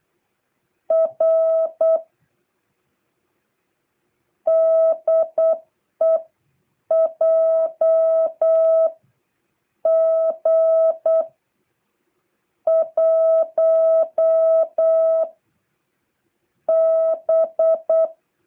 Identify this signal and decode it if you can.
Morse code, heard as a keyed tone, about 8 wpm, spaced as for 6 wpm, 650 Hz. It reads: R DEJG 1 B